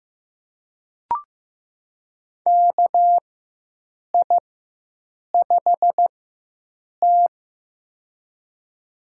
Morse code, heard as a keyed tone, about 16 wpm, spaced as for 7 wpm, 700 Hz. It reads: KI5T
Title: USA Call Signs - 4x Character Spacing - 15wpm